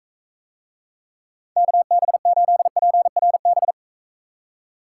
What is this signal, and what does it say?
Morse code, heard as a keyed tone, about 42 wpm, 700 Hz. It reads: KB8PRB